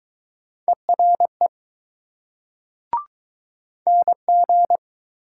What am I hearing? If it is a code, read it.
Morse code, heard as a keyed tone, about 21 wpm, 700 Hz. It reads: ERE NG